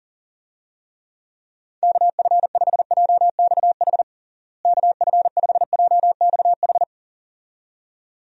Morse code, heard as a keyed tone, about 40 wpm, 700 Hz. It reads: KF5JXH KF5JXH
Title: USA Call Signs - 6 Characters - Repeated 2x - 40wpm